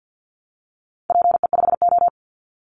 Morse code, heard as a keyed tone, about 50 wpm, 700 Hz. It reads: PE5O
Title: Call Signs - 50wpm